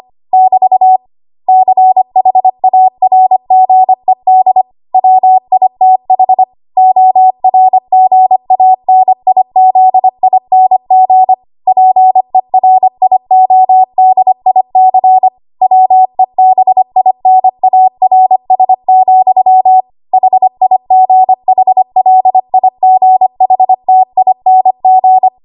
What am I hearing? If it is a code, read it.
Morse code, heard as a keyed tone, about 25 wpm, 750 Hz. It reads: <BT> CHARGED WITH ORGANIZING PERIODIC WEBINARS, HIGHLIGHTING
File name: Archive / 25 WPM / 25WPM